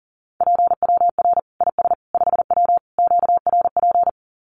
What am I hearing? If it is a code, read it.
Morse code, heard as a keyed tone, about 40 wpm, 700 Hz. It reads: PWR IS 5W QRP
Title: Common QSO Phrases 40wpm